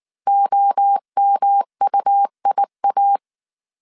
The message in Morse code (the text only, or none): OMUIA